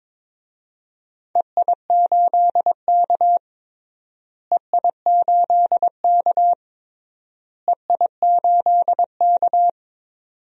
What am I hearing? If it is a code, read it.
Morse code, heard as a keyed tone, about 21 wpm, 700 Hz. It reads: EI8K EI8K EI8K
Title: DXCC-100 Call Signs - 4 Characters - Repeated 3x - 22wpm